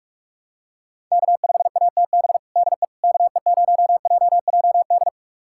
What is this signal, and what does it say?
Morse code, heard as a keyed tone, about 41 wpm, 700 Hz. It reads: K5ATB DE KE0JJD
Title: USA Callsign DE Callsign 45wpm